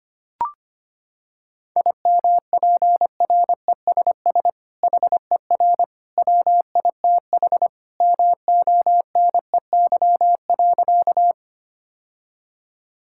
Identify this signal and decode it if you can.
Morse code, heard as a keyed tone, about 25 wpm, 700 Hz. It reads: IMPRESS HER WITH MONEY.
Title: Sentences from Top 1600 Words 25wpm